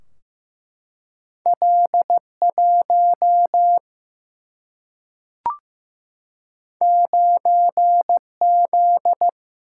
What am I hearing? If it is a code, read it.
Morse code, heard as a keyed tone, about 15 wpm, 700 Hz. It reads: L1 9Z
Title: Single-Letter Single-Number 15wpm